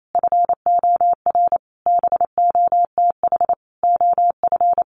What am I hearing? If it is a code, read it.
Morse code, heard as a keyed tone, about 28 wpm, 700 Hz. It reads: FOR BOTH OF
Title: Sentences from Top 1100 Words 28wpm